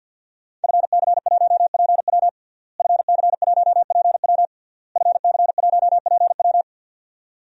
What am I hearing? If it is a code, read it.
Morse code, heard as a keyed tone, about 52 wpm, 700 Hz. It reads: FC1PW FC1PW FC1PW
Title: DXCC-100 Call Signs - 5 Characters - Repeated 3x - 50wpm